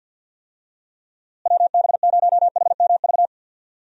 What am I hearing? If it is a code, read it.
Morse code, heard as a keyed tone, about 51 wpm, 700 Hz. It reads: WB0HMV
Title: USA Call Signs - 6 Characters - Repeated 3x - 50wpm